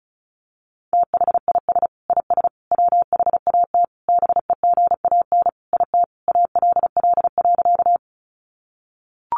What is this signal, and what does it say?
Morse code, heard as a keyed tone, about 34 wpm, 700 Hz. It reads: THIS IS WHAT BEGAN IT ALL.
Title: Sentences from Top 300 Words 35wpm